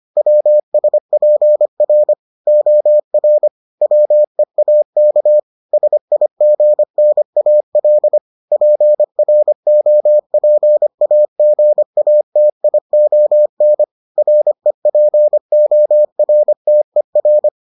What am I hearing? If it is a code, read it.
Morse code, heard as a keyed tone, about 25 wpm, 600 Hz. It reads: WSPR OR WEAK SIGNAL PROPAGATION REPORTER